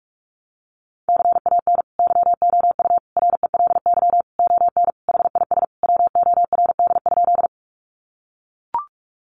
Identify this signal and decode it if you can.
Morse code, heard as a keyed tone, about 44 wpm, 700 Hz. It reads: CAN YOU RELY ON HIS WORD?
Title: Sentences from Top 1200 Words 45wpm